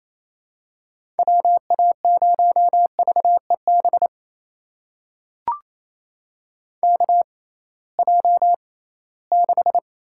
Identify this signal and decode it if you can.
Morse code, heard as a keyed tone, about 27 wpm, spaced as for 18 wpm, 700 Hz. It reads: WA0VEB KJ6
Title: USA Call Signs - 6 Characters - 6x Character Spacing - 28wpm